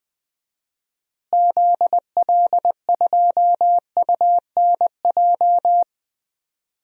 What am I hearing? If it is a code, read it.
Morse code, heard as a keyed tone, about 20 wpm, 700 Hz. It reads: ZL2UNJ
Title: DXCC-100 Call Signs - 6 Characters - Repeated 3x - 20wpm